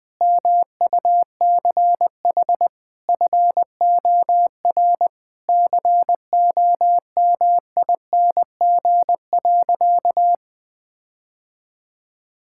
Morse code, heard as a keyed tone, about 20 wpm, 700 Hz. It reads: MUCH FOR COMING.
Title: Sentences with 4-Word Phrases - 20wpm